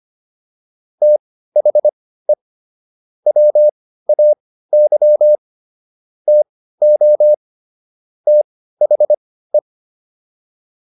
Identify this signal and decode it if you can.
Morse code, heard as a keyed tone, about 25 wpm, spaced as for 15 wpm, 600 Hz. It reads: THE WAY TO THE